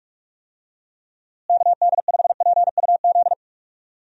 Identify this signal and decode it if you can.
Morse code, heard as a keyed tone, about 46 wpm, 700 Hz. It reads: KD5PUZ
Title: USA Call Signs - 6 Characters - Repeated 2x - 45wpm